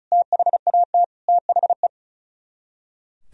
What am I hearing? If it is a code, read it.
Morse code, heard as a keyed tone, about 46 wpm, spaced as for 41 wpm, 700 Hz. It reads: THAT THE
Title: Speed-Racing - Sets of 5 Words 35wpm